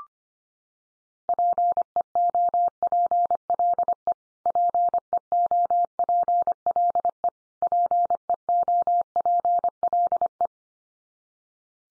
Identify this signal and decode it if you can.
Morse code, heard as a keyed tone, about 25 wpm, 700 Hz. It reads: PEOPLE PEOPLE PEOPLE
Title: Top 100 Words 25wpm x3